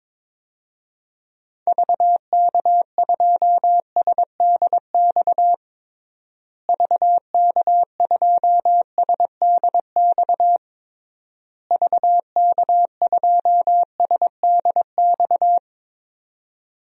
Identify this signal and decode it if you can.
Morse code, heard as a keyed tone, about 22 wpm, 700 Hz. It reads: VK2SDX VK2SDX VK2SDX
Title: DXCC-100 Call Signs - 6 Characters - Repeated 3x - 22wpm